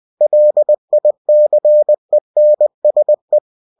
Morse code, heard as a keyed tone, about 20 wpm, 600 Hz. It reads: LICENSE
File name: cw-books / winnie-the-pooh / winnie-20wpm-page-0112